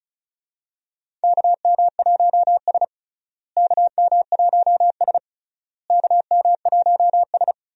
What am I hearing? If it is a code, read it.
Morse code, heard as a keyed tone, about 36 wpm, 700 Hz. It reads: KM1S KM1S KM1S
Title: USA Call Signs - 4 Characters - Repeated 3x - 35wpm